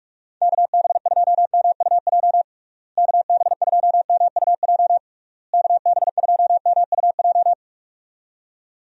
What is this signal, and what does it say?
Morse code, heard as a keyed tone, about 46 wpm, 700 Hz. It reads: KB2MUJ KB2MUJ KB2MUJ
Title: USA Call Signs - 6 Characters - Repeated 3x - 45wpm